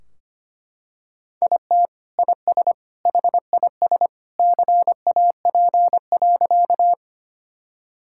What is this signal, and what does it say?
Morse code, heard as a keyed tone, about 25 wpm, 700 Hz. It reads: IT IS HIS CAP.
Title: Sentences from Top 1600 Words 25wpm